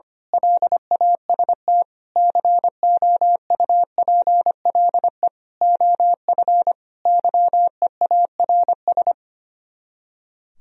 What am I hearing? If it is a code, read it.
Morse code, heard as a keyed tone, about 24 wpm, 700 Hz. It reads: LAST COUPLE OF YEARS
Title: Speed-Racing - Sets of 5 Words 25wpm